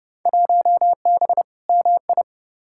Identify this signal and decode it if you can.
Morse code, heard as a keyed tone, about 30 wpm, 700 Hz. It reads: 1B MI
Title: Field Day Exchanges - 30wpm